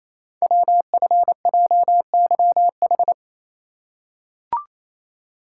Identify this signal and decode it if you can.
Morse code, heard as a keyed tone, about 28 wpm, 690 Hz. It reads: WFJYH
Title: Code Groups - 5 Letters - 28wpm